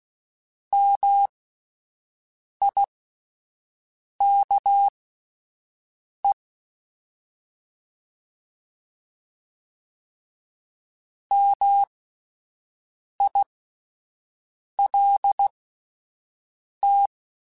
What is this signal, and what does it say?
Morse code, heard as a keyed tone, about 16 wpm, spaced as for 5 wpm, 800 Hz. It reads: MIKE MILT
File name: Alphabetical / Alphabetical mp3s / Ms @5